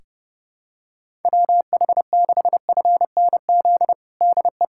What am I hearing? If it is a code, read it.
Morse code, heard as a keyed tone, about 30 wpm, 700 Hz. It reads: WH6FNZ DE
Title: USA Callsign DE Callsign - 6 Characters - 30wpm